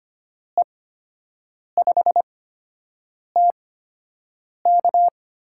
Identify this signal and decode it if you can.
Morse code, heard as a keyed tone, about 25 wpm, spaced as for 7 wpm, 700 Hz. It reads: E5TK
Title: USA Call Signs - 6 Characters - 8x Character Spacing - 25wpm